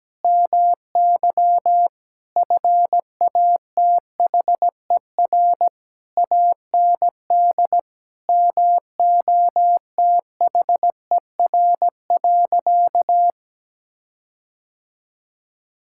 Morse code, heard as a keyed tone, about 17 wpm, 700 Hz. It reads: MY FATHER AND MOTHER.
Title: Sentences from Top 1600 Words 17wpm